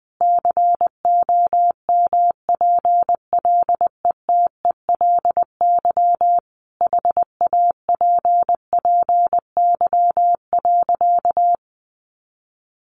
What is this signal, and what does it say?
Morse code, heard as a keyed tone, about 20 wpm, 700 Hz. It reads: COMPLETELY HAPPY.
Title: Sentences from Top 1100 Words 20wpm